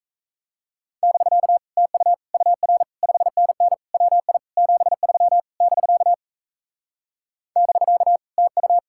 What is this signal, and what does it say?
Morse code, heard as a keyed tone, about 38 wpm, 700 Hz. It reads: <BK> TU UR 5NN WI 73 <BK> <BK> TU